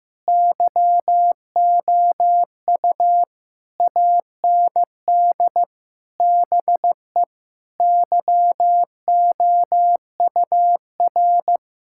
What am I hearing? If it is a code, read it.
Morse code, heard as a keyed tone, about 16 wpm, 700 Hz. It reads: YOU AND BE YOUR
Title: Sentences from Top 1600 Words 15wpm